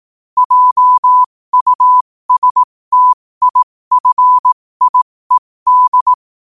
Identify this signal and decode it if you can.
Morse code, heard as a keyed tone, about 18 wpm, spaced as for 15 wpm, 1000 Hz. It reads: JUSTIFIED
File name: qotdfiles / 15wpm